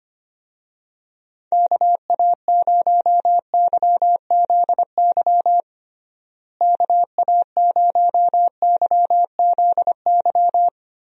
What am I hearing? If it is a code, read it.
Morse code, heard as a keyed tone, about 25 wpm, 700 Hz. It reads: KA0YZY KA0YZY
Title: USA Call Signs - 6 Characters - Repeated 3x - 25wpm